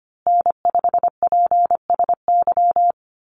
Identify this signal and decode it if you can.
Morse code, heard as a keyed tone, about 25 wpm, 700 Hz. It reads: N5PSY
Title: USA Call Signs Repeated 2x 25wpm